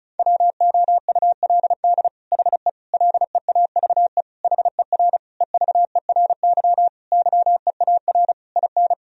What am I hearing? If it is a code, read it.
Morse code, heard as a keyed tone, about 36 wpm, 700 Hz. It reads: WOULD HE LEAVE HER EVERY YEAR IN
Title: Sentences from Top 1000 Words - Review Part 1 - 35wpm